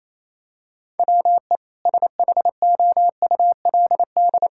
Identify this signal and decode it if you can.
Morse code, heard as a keyed tone, about 28 wpm, 700 Hz. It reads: WE SHOULD
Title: Sentences from Top 1000 Words - Review Part 5 - 28wpm